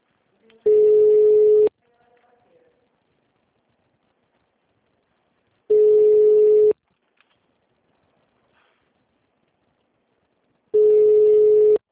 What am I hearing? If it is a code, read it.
Ringback tone (Europe, 425 Hz), with speech.
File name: out- -2003- - - .0